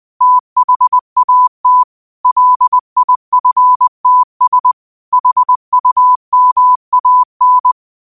Morse code, heard as a keyed tone, about 20 wpm, 1000 Hz. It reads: THAT LIFTS HUMAN